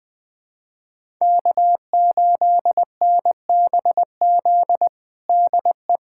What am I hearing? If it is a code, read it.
Morse code, heard as a keyed tone, about 20 wpm, 700 Hz. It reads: K8NBZ DE
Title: USA Callsign DE Callsign - 5 Characters - 20wpm